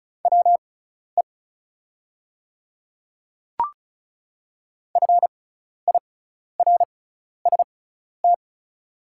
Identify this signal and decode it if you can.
Morse code, heard as a keyed tone, about 34 wpm, spaced as for 12 wpm, 690 Hz. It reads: WE FIRST
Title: 6x Character Spacing - Top 100 Words - 35wpm